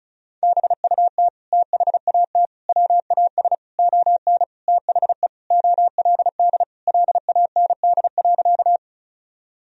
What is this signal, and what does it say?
Morse code, heard as a keyed tone, about 36 wpm, 700 Hz. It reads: BUT THAT WAS ON THE OLD LAND.
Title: Sentences from Top 1000 Words - Review Part 1 - 35wpm